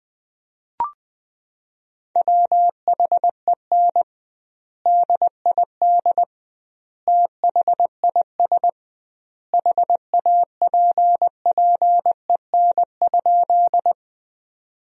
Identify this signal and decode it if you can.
Morse code, heard as a keyed tone, about 21 wpm, 700 Hz. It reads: WHEN DID THIS HAPPEN?
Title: 2x Word-Spacing - Sentences from Top 400 Words - 20wpm